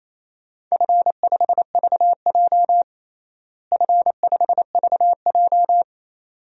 Morse code, heard as a keyed tone, about 28 wpm, 700 Hz. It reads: F5VJ F5VJ
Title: DXCC-100 Call Signs - 4 Characters - Repeated 2x - 28wpm